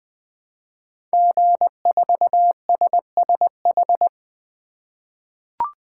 Morse code, heard as a keyed tone, about 20 wpm, 700 Hz. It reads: G4SSH
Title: 2023 SOTA Chaser Call Signs - 20wpm